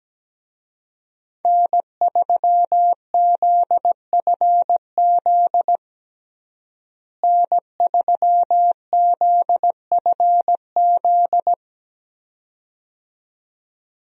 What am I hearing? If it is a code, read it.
Morse code, heard as a keyed tone, about 17 wpm, 700 Hz. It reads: N3ZFZ N3ZFZ
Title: USA Call Signs - 5 Characters - Repeated 2x - 17wpm